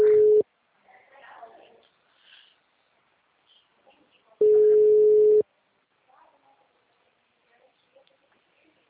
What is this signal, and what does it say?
Ringback tone (Europe, 425 Hz), with speech.